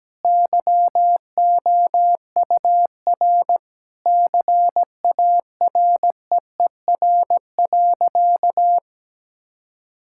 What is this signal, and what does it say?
Morse code, heard as a keyed tone, about 17 wpm, 700 Hz. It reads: YOUR CAREER.
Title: Sentences from Top 1000 Words - Review Part 6 - 17wpm